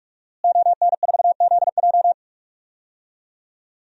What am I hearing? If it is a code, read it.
Morse code, heard as a keyed tone, about 46 wpm, 700 Hz. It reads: ON4ZJ
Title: DXCC-100 Call Signs - 5 Characters - Repeated 2x - 45wpm